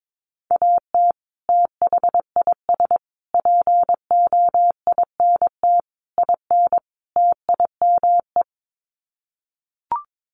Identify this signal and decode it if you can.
Morse code, heard as a keyed tone, about 22 wpm, 700 Hz. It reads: AT THIS POINT IN TIME
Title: Sets of 5 Words Encore 22wpm